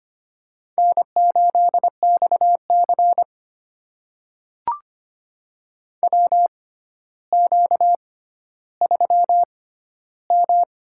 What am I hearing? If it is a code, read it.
Morse code, heard as a keyed tone, about 25 wpm, spaced as for 13 wpm, 700 Hz. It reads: N8XC WQ3M
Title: USA Call Signs - 4 Characters - 6x Character Spacing - 25wpm